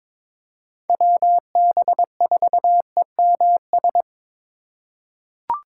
Morse code, heard as a keyed tone, about 22 wpm, 700 Hz. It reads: WB4EMS
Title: USA Call Signs - 2x Character Spacing - 22wpm